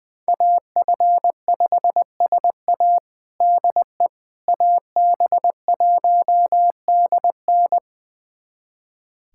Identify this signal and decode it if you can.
Morse code, heard as a keyed tone, about 20 wpm, 700 Hz. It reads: AF5SA DE AB1DN
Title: USA Callsign DE Callsign - 5 Characters - 20wpm